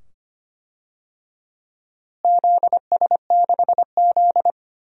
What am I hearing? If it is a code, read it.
Morse code, heard as a keyed tone, about 25 wpm, 700 Hz. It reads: ZS6Z
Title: DXCC-100 Call Signs - 4 Characters - Repeated 2x - 25wpm